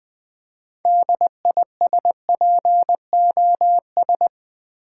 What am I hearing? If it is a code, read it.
Morse code, heard as a keyed tone, about 20 wpm, 700 Hz. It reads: DISPOS
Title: Build A Word - 2 Letters At A Time - 8-Letter Words - 20wpm